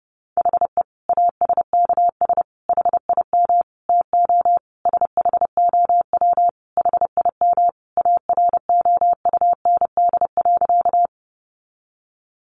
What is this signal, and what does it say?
Morse code, heard as a keyed tone, about 30 wpm, 700 Hz. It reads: HE ASKS HIM TO SHOW HIM AROUND.
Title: Easy Phrases 30wpm